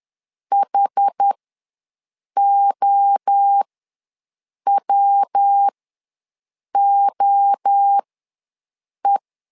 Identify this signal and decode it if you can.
Morse code, heard as a keyed tone, about 10 wpm, spaced as for 6 wpm, 800 Hz. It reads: HOWOE
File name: LEZIONI / 3 MWRO / 3 MWRO ws 10wpm 2310